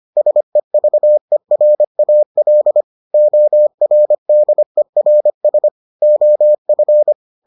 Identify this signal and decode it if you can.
Morse code, heard as a keyed tone, about 25 wpm, 600 Hz. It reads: SEVERAL ORDERS OF